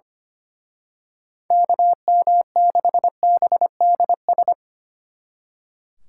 Morse code, heard as a keyed tone, about 29 wpm, 700 Hz. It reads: KM6BDS
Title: Speed Racing - USA 2x3 Call Signs 25wpm